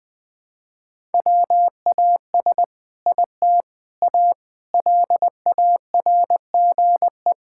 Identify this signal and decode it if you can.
Morse code, heard as a keyed tone, about 20 wpm, 700 Hz. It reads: WAS IT A LARGE
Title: Sentences from Top 1000 Words - Review Part 5 - 20wpm